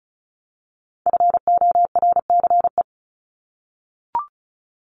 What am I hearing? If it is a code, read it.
Morse code, heard as a keyed tone, about 34 wpm, 700 Hz. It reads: FORCE
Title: ED Suffix - Top 100 Words with Base Words 35wpm